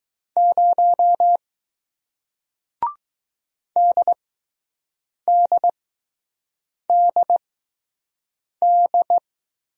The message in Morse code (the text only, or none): 0 DDDD